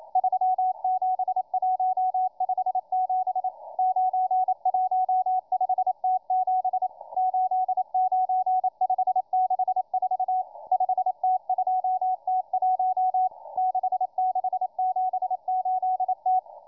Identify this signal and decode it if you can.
Morse code, heard as a keyed tone, about 27 wpm, 740 Hz, with radio noise. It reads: 37157 915T7 89564 5T2T1 6678T